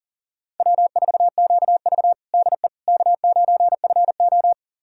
Morse code, heard as a keyed tone, about 40 wpm, 700 Hz. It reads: W4QV DE K9FO
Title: USA Callsign DE Callsign - 4 Characters - 40wpm